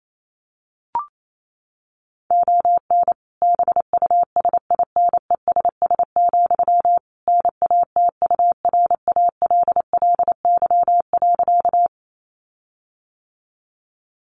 Morse code, heard as a keyed tone, about 28 wpm, 700 Hz. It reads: ON BUSINESS, NATURALLY.
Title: Sentences from Top 1300 Words 28wpm